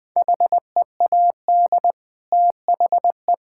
Morse code, heard as a keyed tone, about 20 wpm, 700 Hz. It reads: HEAD THE
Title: Sentences from Top 1000 Words - Review Part 5 - 20wpm